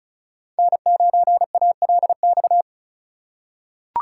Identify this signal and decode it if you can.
Morse code, heard as a keyed tone, about 36 wpm, 700 Hz. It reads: N9ALX
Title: USA Call Signs - 5 Characters - 2x Character Spacing - 35wpm